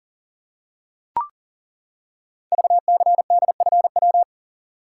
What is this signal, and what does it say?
Morse code, heard as a keyed tone, about 38 wpm, 700 Hz. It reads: VCDFW
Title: Code Groups - 5 Letters - 40wpm